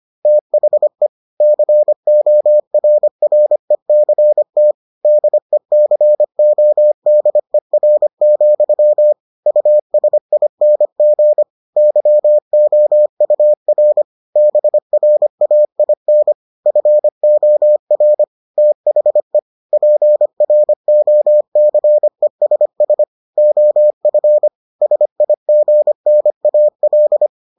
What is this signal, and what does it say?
Morse code, heard as a keyed tone, about 25 wpm, 600 Hz. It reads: THE CORRECT DECODER, USING YOUR BRAIN FOR THE PROCESS OF SIGNAL